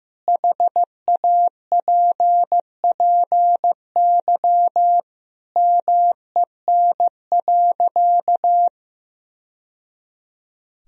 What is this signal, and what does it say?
Morse code, heard as a keyed tone, about 15 wpm, 700 Hz. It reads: HAPPY MEN.
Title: Sentences from Top 1000 Words - Review Part 1 - 15wpm